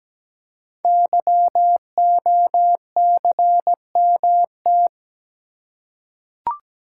Morse code, heard as a keyed tone, about 17 wpm, 700 Hz. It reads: YOCMT
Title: Code Groups - 5 Letters - 17wpm